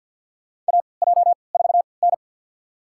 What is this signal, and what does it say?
Morse code, heard as a keyed tone, about 52 wpm, spaced as for 29 wpm, 700 Hz. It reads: AJ4N
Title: USA Call Signs - 4 Characters - 3x Character Spacing - 50wpm